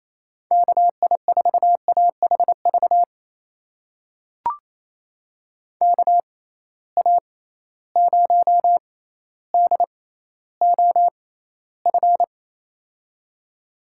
Morse code, heard as a keyed tone, about 27 wpm, spaced as for 14 wpm, 700 Hz. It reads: KI4AHV KA0DOF
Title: USA Call Signs - 6 Characters - 6x Character Spacing - 28wpm